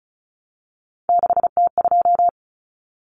Morse code, heard as a keyed tone, about 34 wpm, 700 Hz. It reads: 6T2
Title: Common Cut Numbers - 3-Digit 35wpm